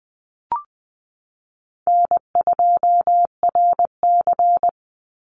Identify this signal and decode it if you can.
Morse code, heard as a keyed tone, about 20 wpm, 700 Hz. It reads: N2RC
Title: Easy Call Signs Repeated 2x - 20wpm